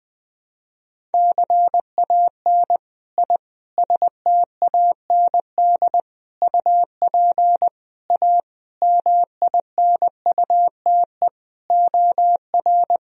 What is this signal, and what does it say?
Morse code, heard as a keyed tone, about 20 wpm, 700 Hz. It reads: CAN I STAND UP A MINUTE OR
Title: Sentences from Top 1000 Words - Review Part 1 - 20wpm